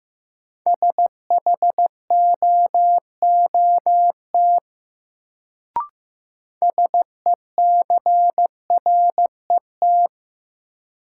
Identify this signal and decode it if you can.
Morse code, heard as a keyed tone, about 15 wpm, 700 Hz. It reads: SHOOT SECRET
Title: Sentences from Top 1200 Words - New Word Drill - 15wpm